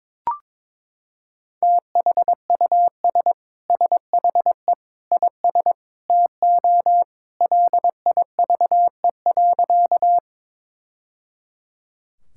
Morse code, heard as a keyed tone, about 22 wpm, 700 Hz. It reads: THUS SHE IS TO LIVE.
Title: Sentences from Top 1000 Words - Review Part 5 - 22wpm